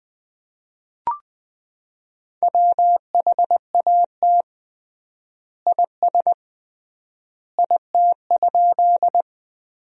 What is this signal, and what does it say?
Morse code, heard as a keyed tone, about 21 wpm, 700 Hz. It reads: WHAT IS IT?
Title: 3x Word-Spacing - Sentences from Top 100 Words - 20wpm